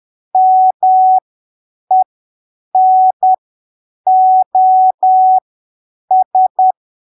Morse code, heard as a keyed tone, about 10 wpm, spaced as for 7 wpm, 750 Hz. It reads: MENOS
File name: arquivos / texto / texto 05 10ppm